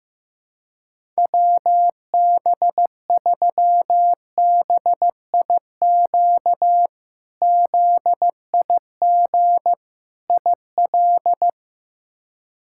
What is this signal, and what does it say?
Morse code, heard as a keyed tone, about 15 wpm, 700 Hz. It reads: WB3BIQ ZIG IL
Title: K1USN SST QSO Practice - 15wpm